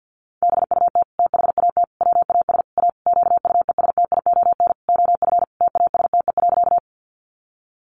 Morse code, heard as a keyed tone, about 50 wpm, 700 Hz. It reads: BUT THAT WAS A QUESTION OF TASTE.
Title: Sentences from Top 1100 Words 50wpm